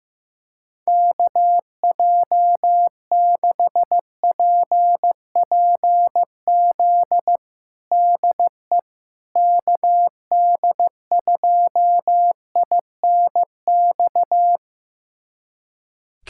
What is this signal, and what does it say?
Morse code, heard as a keyed tone, about 15 wpm, 700 Hz. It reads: KJ6PPZ DE KD2INX
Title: USA Callsign DE Callsign 15wpm